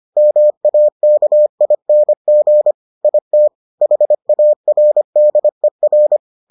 Morse code, heard as a keyed tone, about 25 wpm, 600 Hz. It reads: MAKING IT HARDER